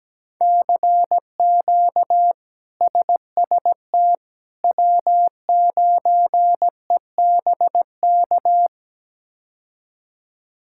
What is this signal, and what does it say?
Morse code, heard as a keyed tone, about 17 wpm, 700 Hz. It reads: CQ SST W9EBK
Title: K1USN SST QSO Practice - 17wpm